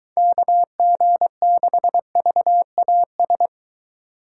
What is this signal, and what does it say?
Morse code, heard as a keyed tone, about 24 wpm, 700 Hz. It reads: KG6VAS